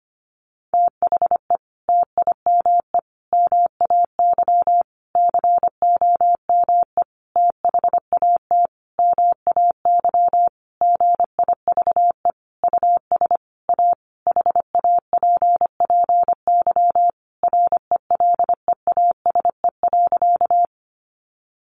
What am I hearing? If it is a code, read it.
Morse code, heard as a keyed tone, about 25 wpm, 700 Hz. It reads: THE TIME MAY COME THAT MAY GIVE US A HAPPY RELEASE.
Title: Sentences from Top 1100 Words 25wpm